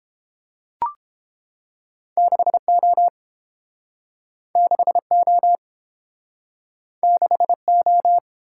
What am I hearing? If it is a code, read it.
Morse code, heard as a keyed tone, about 30 wpm, 700 Hz. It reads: 6O 6O 6O